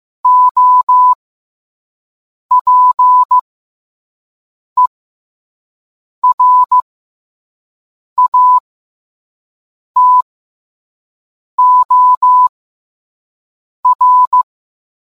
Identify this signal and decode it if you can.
Morse code, heard as a keyed tone, about 15 wpm, spaced as for 5 wpm, 1000 Hz. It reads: OPERATOR